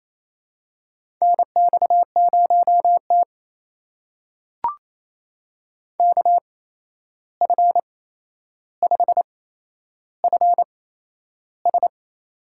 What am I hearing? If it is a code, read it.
Morse code, heard as a keyed tone, about 28 wpm, spaced as for 11 wpm, 700 Hz. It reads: NX0T KF5FS
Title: USA Call Signs - 8x Character Spacing - 28wpm